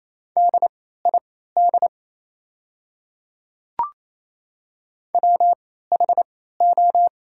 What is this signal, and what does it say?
Morse code, heard as a keyed tone, about 28 wpm, spaced as for 16 wpm, 690 Hz. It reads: DID WHO
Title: 3x Character Spacing - Top 100 Words - 28wpm